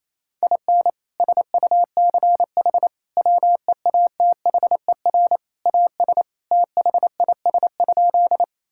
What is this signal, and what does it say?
Morse code, heard as a keyed tone, about 28 wpm, 700 Hz. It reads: IN SUCH WEATHER AS THIS?
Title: Sentences from Top 1000 Words - Review Part 5 - 28wpm